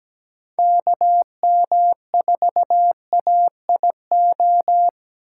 Morse code, heard as a keyed tone, about 17 wpm, 700 Hz. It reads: KM4AIO
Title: USA Call Signs - 4x Character Spacing - 17wpm